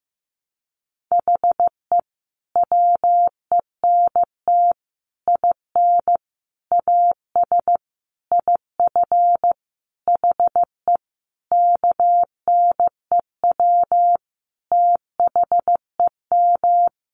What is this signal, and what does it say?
Morse code, heard as a keyed tone, about 15 wpm, 700 Hz. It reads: HE WENT IN AS IF HE KNEW THEM
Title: Sentences from Top 1400 Words 15wpm